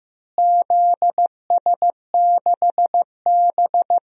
Morse code, heard as a keyed tone, about 15 wpm, 700 Hz. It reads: ZS6B
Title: DXCC-100 Call Signs - 4 Characters - Repeated 3x - 15wpm